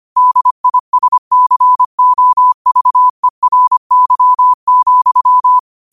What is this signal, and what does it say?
Morse code, heard as a keyed tone, about 25 wpm, 1000 Hz. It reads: DISCOVERY,
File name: qotdfiles / 25wpm